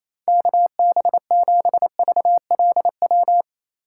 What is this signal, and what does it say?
Morse code, heard as a keyed tone, about 28 wpm, 690 Hz. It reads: KB7VLW
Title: USA Call Signs - 6 Characters - 4x Character Spacing - 28wpm